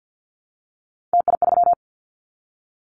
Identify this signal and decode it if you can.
Morse code, heard as a keyed tone, about 50 wpm, 700 Hz. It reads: TI3
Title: Callsign Prefixes for Top 100 DXCC - 50wpm